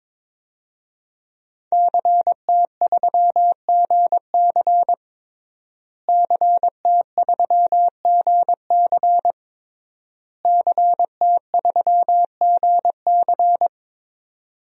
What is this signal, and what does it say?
Morse code, heard as a keyed tone, about 22 wpm, 700 Hz. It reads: CT3GC CT3GC CT3GC
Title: DXCC-100 Call Signs - 5 Characters - Repeated 3x - 22wpm